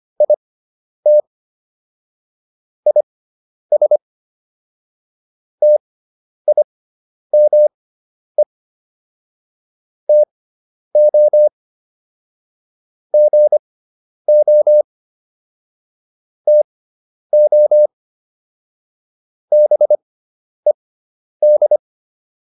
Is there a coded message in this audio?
Yes: Morse code from 0.2 to 21.8 s, heard as a keyed tone, about 25 wpm, spaced as for 10 wpm, 600 Hz. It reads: IT IS TIME TO GO TO BED